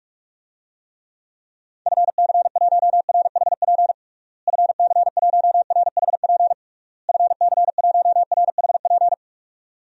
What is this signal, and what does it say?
Morse code, heard as a keyed tone, about 46 wpm, 700 Hz. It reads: FC1RHP FC1RHP FC1RHP
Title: DXCC-100 Call Signs - 6 Characters - Repeated 3x - 45wpm